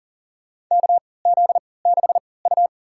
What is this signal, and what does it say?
Morse code, heard as a keyed tone, about 40 wpm, spaced as for 23 wpm, 700 Hz. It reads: KZ6U
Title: USA Call Signs - 3x Character Spacing - 40wpm